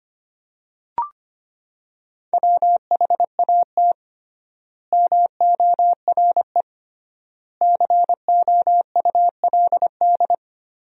Morse code, heard as a keyed tone, about 25 wpm, 700 Hz. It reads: WHAT MORE COULD
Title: 3x Word-Spacing - Sentences from Top 200 Words - 25wpm